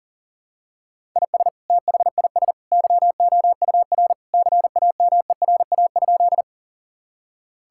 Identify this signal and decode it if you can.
Morse code, heard as a keyed tone, about 40 wpm, 700 Hz. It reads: IS THIS YOUR CAMERA?
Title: Sentences from Top 1000 Words - Review Part 5 - 40wpm